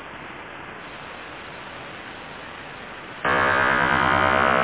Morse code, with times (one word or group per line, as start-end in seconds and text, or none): none